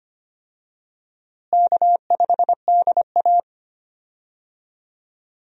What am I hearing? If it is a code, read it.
Morse code, heard as a keyed tone, about 25 wpm, 700 Hz. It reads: K5DA
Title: USA Call Signs - 4 Characters - Repeated 2x - 25wpm